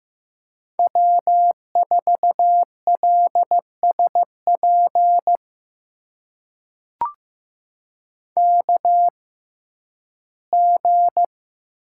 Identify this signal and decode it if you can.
Morse code, heard as a keyed tone, about 15 wpm, spaced as for 11 wpm, 700 Hz. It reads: W4LSP KG
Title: USA Call Signs - 6x Character Spacing - 15wpm